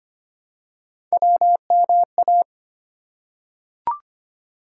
Morse code, heard as a keyed tone, about 25 wpm, 700 Hz. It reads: WMA
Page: morse-code-ninja-practice-25wpm